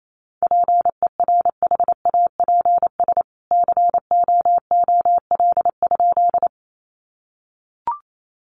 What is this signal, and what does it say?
Morse code, heard as a keyed tone, about 28 wpm, 700 Hz. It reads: PERHAPS COOL?
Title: Sentences from Top 1200 Words 28wpm